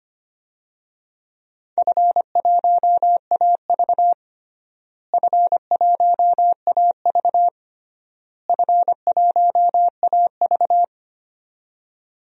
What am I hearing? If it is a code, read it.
Morse code, heard as a keyed tone, about 25 wpm, 700 Hz. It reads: F1AV F1AV F1AV
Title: DXCC-100 Call Signs - 4 Characters - Repeated 3x - 25wpm